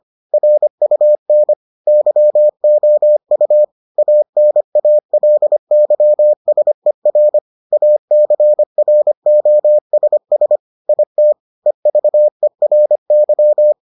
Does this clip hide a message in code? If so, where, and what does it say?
Yes: Morse code from 0.3 to 13.7 s, heard as a keyed tone, about 25 wpm, 600 Hz. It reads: RUN YOU ANALYSER ACROSS IT EVERY